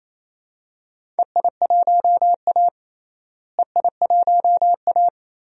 Morse code, heard as a keyed tone, about 28 wpm, 700 Hz. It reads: EI1A EI1A
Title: DXCC-100 Call Signs - 4 Characters - Repeated 3x - 28wpm